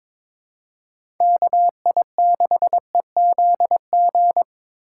Morse code, heard as a keyed tone, about 22 wpm, 700 Hz. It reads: KI6EZG
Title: USA Call Signs - 6 Characters - 2x Character Spacing - 22wpm